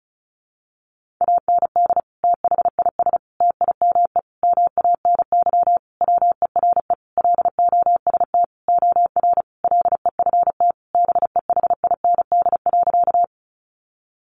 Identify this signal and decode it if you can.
Morse code, heard as a keyed tone, about 34 wpm, 700 Hz. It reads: AND THIS TIME MANY WERE LOST OR LEFT BEHIND.
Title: Sentences from Top 1400 Words 35wpm